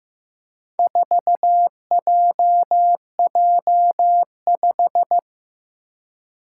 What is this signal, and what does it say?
Morse code, heard as a keyed tone, about 15 wpm, 700 Hz. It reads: 4JJ5
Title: Four Challenging Letter-Number Combinations 15wpm